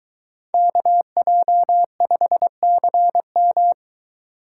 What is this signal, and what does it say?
Morse code, heard as a keyed tone, about 22 wpm, 700 Hz. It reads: KJ5CM